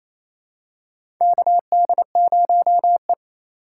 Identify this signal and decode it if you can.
Morse code, heard as a keyed tone, about 28 wpm, 700 Hz. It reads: KD0E
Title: USA Call Signs - 3x Character Spacing - 28wpm